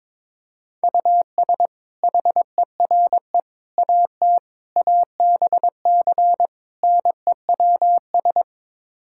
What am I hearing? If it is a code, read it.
Morse code, heard as a keyed tone, about 22 wpm, 700 Hz. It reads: US HERE AT ABC NEWS
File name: practice / 22wpm / Build A Phrase - 5-Word Phrases - 22wpm